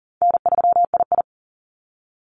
Morse code, heard as a keyed tone, about 40 wpm, 700 Hz. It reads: N3II
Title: NA SOTA Activator Callsigns 40wpm